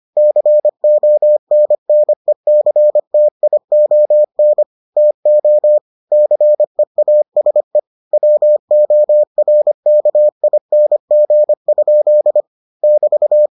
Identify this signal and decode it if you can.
Morse code, heard as a keyed tone, about 25 wpm, 600 Hz. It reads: CONNECTION TO CEASE WORKING? <BT>